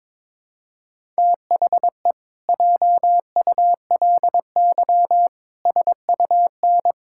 Morse code, heard as a keyed tone, about 22 wpm, 700 Hz. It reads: THE JULY SUN
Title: Pangram Challenge v2 - 22wpm